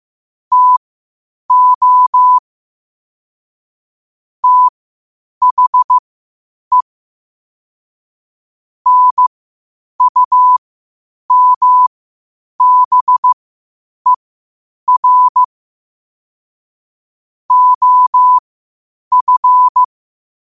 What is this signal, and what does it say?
Morse code, heard as a keyed tone, about 15 wpm, spaced as for 8 wpm, 1000 Hz. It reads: TO THE NUMBER OF
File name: qotdfiles / 8wpm